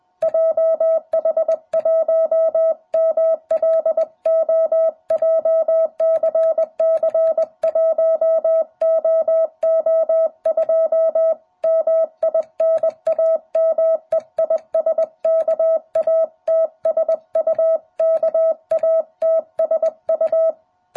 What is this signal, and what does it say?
Morse code, heard as a keyed tone, about 20 wpm, 660 Hz. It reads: JH1MLO JCC1OO2 MI NAME IS KATSU KATSU